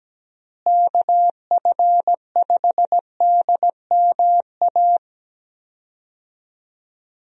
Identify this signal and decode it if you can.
Morse code, heard as a keyed tone, about 17 wpm, 700 Hz. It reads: KF5DMA
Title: USA Call Signs - 6 Characters - Repeated 3x - 17wpm